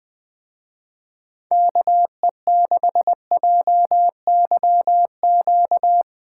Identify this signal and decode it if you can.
Morse code, heard as a keyed tone, about 20 wpm, 700 Hz. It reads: KE6JYQ
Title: Speed Racing - USA Call Signs 17wpm